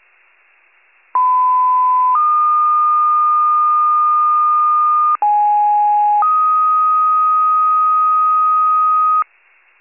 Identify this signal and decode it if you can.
Two-tone page, heard as a one-second tone, then a three-second tone, about 1010 then 1220 Hz, about 830 then 1250 Hz. It is heard from 1.1 to 9.2 s.